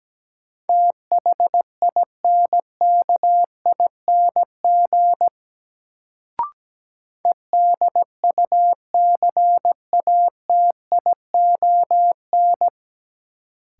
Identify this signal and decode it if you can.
Morse code, heard as a keyed tone, about 17 wpm, 700 Hz. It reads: THINKING EDUCATION
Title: Sentences from Top 1000 Words - Review Part 6 - New Word Drill - 17wpm